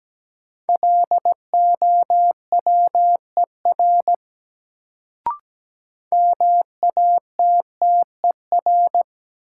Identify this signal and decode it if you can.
Morse code, heard as a keyed tone, about 17 wpm, 700 Hz. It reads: LOWER MATTER
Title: Sentences from Top 700 Words - New Word Drill - 17wpm